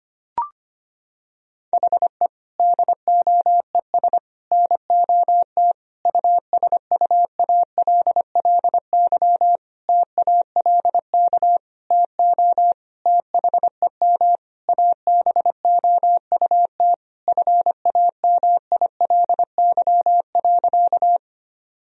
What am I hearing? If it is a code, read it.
Morse code, heard as a keyed tone, about 25 wpm, 700 Hz. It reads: HE DOES NOT USUALLY TALK TO THEM ABOUT FAMILY.
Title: Sentences from Top 1000 Words - Review Part 3 - 25wpm